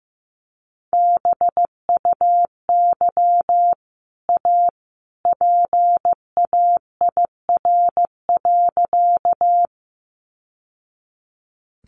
Morse code, heard as a keyed tone, about 15 wpm, 700 Hz. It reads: BUY A PAIR.
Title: Sentences from Top 1200 Words 15wpm